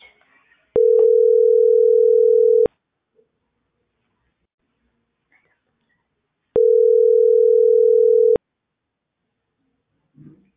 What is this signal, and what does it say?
Ringback tone (North America), 440 + 480 Hz.